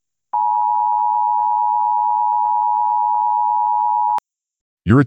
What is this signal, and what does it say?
Emergency Alert System (SAME) tones (the 853 and 960 Hz attention tone). It is heard from 0.3 to 4.2 s.